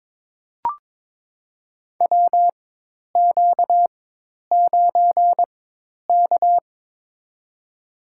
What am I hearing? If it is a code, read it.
Morse code, heard as a keyed tone, about 22 wpm, spaced as for 10 wpm, 700 Hz. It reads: WQ9K
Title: USA Call Signs - 4 Characters - 4x Character Spacing - 22wpm